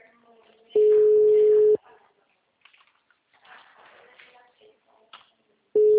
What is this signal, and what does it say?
Ringback tone (Europe, 425 Hz), with speech.
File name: out- -2036- - - .484